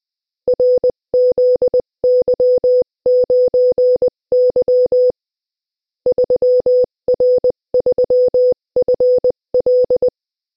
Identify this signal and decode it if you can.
Morse code, heard as a keyed tone, about 20 wpm, 500 Hz. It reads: RZY9Y 3R3FL
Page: Приём на слух сигналов кода Морзе - Тестовое задание 3